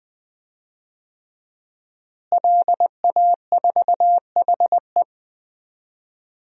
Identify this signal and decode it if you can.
Morse code, heard as a keyed tone, about 20 wpm, 700 Hz. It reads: LA4HE
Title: DXCC-100 Call Signs - 5 Characters - Repeated 3x - 20wpm